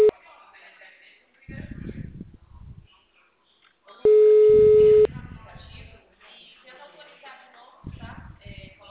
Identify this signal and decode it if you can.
Ringback tone (Europe, 425 Hz), with speech.